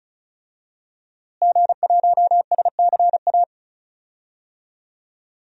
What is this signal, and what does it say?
Morse code, heard as a keyed tone, about 34 wpm, 700 Hz. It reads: G1SCA
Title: DXCC-100 Call Signs - 5 Characters - Repeated 3x - 35wpm